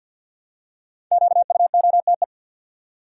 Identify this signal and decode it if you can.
Morse code, heard as a keyed tone, about 51 wpm, 700 Hz. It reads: QUOTE
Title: Five Letter Words - 50wpm